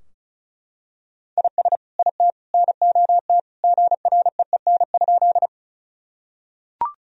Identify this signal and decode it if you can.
Morse code, heard as a keyed tone, about 34 wpm, 700 Hz. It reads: IS IT NOT GREEN?
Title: Sentences from Top 1000 Words - Review Part 2 - 35wpm